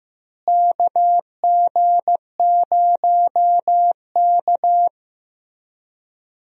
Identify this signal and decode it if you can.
Morse code, heard as a keyed tone, about 15 wpm, 700 Hz. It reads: KG0K
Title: USA Call Signs - 4 Characters - Repeated 2x - 15wpm